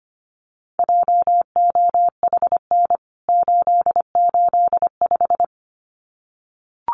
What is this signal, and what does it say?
Morse code, heard as a keyed tone, about 25 wpm, 700 Hz. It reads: JOHN 885
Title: CWT Exchange 25wpm